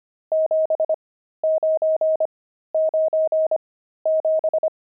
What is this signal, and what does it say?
Morse code, heard as a keyed tone, about 25 wpm, spaced as for 13 wpm, 630 Hz. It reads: 7997
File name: pangram-fs-13-cs-25-nums5